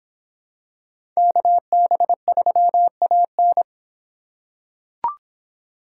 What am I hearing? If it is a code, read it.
Morse code, heard as a keyed tone, about 25 wpm, 700 Hz. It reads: KB3AN